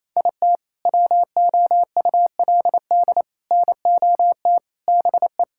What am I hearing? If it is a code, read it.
Morse code, heard as a keyed tone, about 28 wpm, 700 Hz. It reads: IT WOULD NOT BE
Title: Sentences from Top 1600 Words 28wpm